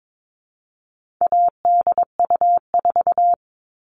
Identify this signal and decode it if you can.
Morse code, heard as a keyed tone, about 22 wpm, 700 Hz. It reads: ADU4
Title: All Cut Numbers - 4-Digit 22wpm